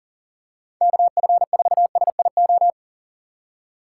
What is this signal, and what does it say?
Morse code, heard as a keyed tone, about 40 wpm, 700 Hz. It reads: KF4SIO
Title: USA Call Signs - 3x Character Spacing - 40wpm